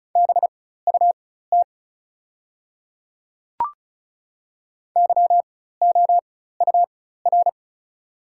Morse code, heard as a keyed tone, about 34 wpm, spaced as for 16 wpm, 700 Hz. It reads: BUT YOUR
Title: 4x Character Spacing - Top 500 Words - 35wpm